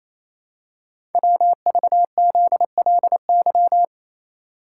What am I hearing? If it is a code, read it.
Morse code, heard as a keyed tone, about 27 wpm, 700 Hz. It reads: WVZLY
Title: Code Groups - 5 Letters - 28wpm